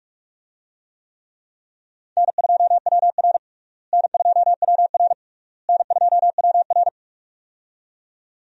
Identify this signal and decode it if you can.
Morse code, heard as a keyed tone, about 45 wpm, 700 Hz. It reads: N2WR N2WR N2WR
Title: USA Call Signs - 4 Characters - Repeated 3x - 45wpm